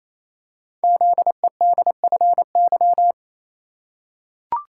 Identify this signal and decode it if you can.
Morse code, heard as a keyed tone, about 27 wpm, 700 Hz. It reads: ZEDFY
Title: Code Groups - 5 Letters - 28wpm